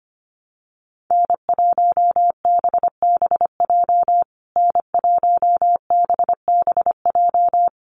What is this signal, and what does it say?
Morse code, heard as a keyed tone, about 25 wpm, 700 Hz. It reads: N1BBJ N1BBJ
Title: USA Call Signs Repeated 3x 25wpm